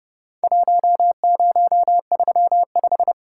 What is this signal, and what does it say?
Morse code, heard as a keyed tone, about 32 wpm, 700 Hz. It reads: 1035
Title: Worked All Europe DX Contest - QTC Exchange - 30wpm